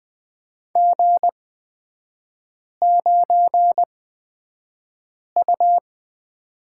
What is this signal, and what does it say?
Morse code, heard as a keyed tone, about 20 wpm, spaced as for 5 wpm, 690 Hz. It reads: G9U
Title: 255 Lesson 41e - Period - Three Characters 20wpm